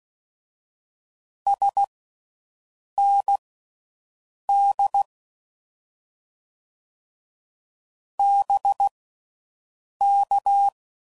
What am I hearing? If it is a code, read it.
Morse code, heard as a keyed tone, about 16 wpm, spaced as for 6 wpm, 800 Hz. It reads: SND BK